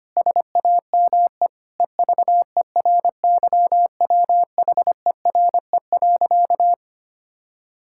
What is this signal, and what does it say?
Morse code, heard as a keyed tone, about 25 wpm, 700 Hz. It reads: SAME EVERYWHERE.
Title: Sentences from Top 1600 Words 25wpm